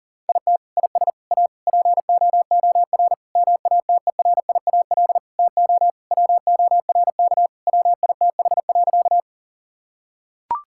Morse code, heard as a keyed tone, about 40 wpm, 700 Hz. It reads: IT IS A POOR MATERIAL TO WORK WITH.
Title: Sentences from Top 1000 Words - Review Part 5 - 40wpm